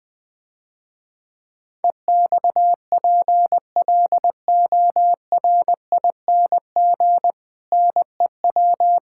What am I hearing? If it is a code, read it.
Morse code, heard as a keyed tone, about 20 wpm, 700 Hz. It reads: EXPLORING NEW
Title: Difficult Sentences with Hints - 20wpm